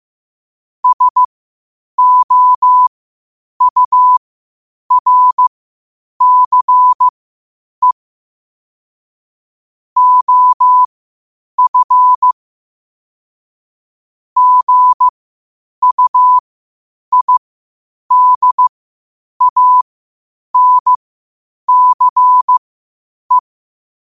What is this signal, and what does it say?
Morse code, heard as a keyed tone, about 15 wpm, spaced as for 8 wpm, 1000 Hz. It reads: SOURCE OF GUIDANCE
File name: qotdfiles / 8wpm